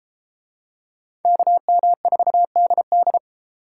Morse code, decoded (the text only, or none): KM4DD